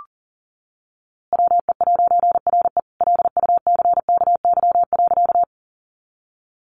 Morse code, heard as a keyed tone, about 37 wpm, 700 Hz. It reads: WE'RE LUCKY.
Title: A Sound Of Thunder 40wpm